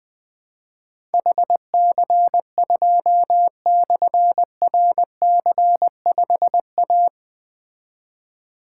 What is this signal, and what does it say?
Morse code, heard as a keyed tone, about 20 wpm, 700 Hz. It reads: HC2/RC5A
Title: Difficult Call Signs 20wpm